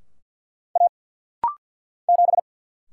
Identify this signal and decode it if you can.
Morse code, heard as a keyed tone, about 50 wpm, 700 Hz. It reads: A 7
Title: Single Letter-Number - Rapid-Fire 50wpm